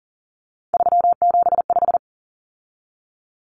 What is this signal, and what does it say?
Morse code, heard as a keyed tone, about 40 wpm, 700 Hz. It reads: 375
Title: Common Cut Numbers - 3-Digit 40wpm